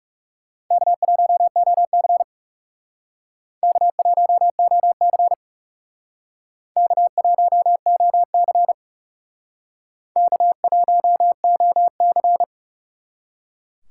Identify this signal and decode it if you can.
Morse code, heard as a keyed tone, about 36 wpm, 700 Hz. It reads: K1OC K1OC K1OC K1OC
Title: Speed Racing - USA 1x2 Call Signs 30wpm